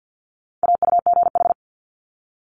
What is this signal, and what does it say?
Morse code, heard as a keyed tone, about 50 wpm, 700 Hz. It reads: AUGH
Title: Multi-letter Phonemes Quick Fire 50wpm